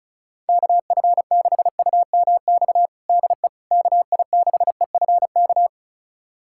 Morse code, heard as a keyed tone, about 36 wpm, 700 Hz. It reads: KF6UMX DE KI6EFK
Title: USA Callsign DE Callsign - 6 Characters - 35wpm